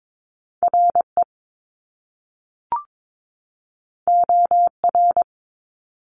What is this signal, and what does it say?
Morse code, heard as a keyed tone, about 22 wpm, 700 Hz. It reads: RE OR
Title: Top Two Letter Combinations in Words 22wpm